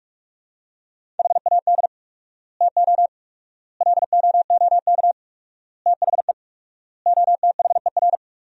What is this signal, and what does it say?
Morse code, heard as a keyed tone, about 46 wpm, 700 Hz. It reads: HAD TO LOOK THE OTHER
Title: 4x Word-Spacing - Sentences from Top 100 Words - 45wpm